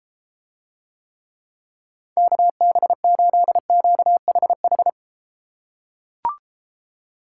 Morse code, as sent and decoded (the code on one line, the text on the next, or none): -.- -... ---.. --.- .... ....
KB8QHH